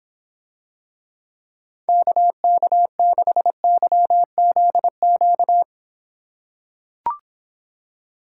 Morse code, heard as a keyed tone, about 26 wpm, 700 Hz. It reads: KK6YZQ